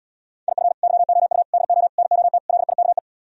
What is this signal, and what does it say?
Morse code, heard as a keyed tone, about 77 wpm, 700 Hz. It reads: IF YOU DO NOT LIKE